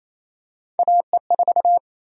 Morse code, heard as a keyed tone, about 28 wpm, 700 Hz. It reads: AE4
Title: USA Call Sign Prefixes - 28wpm